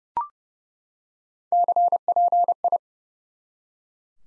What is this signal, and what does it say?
Morse code, heard as a keyed tone, about 30 wpm, 700 Hz. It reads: CPI
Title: Top 90 Abbr 30WPM